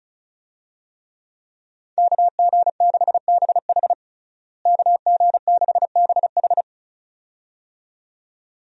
Morse code, heard as a keyed tone, about 34 wpm, 700 Hz. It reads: KG6BH KG6BH
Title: USA Call Signs - 5 Characters - Repeated 2x - 35wpm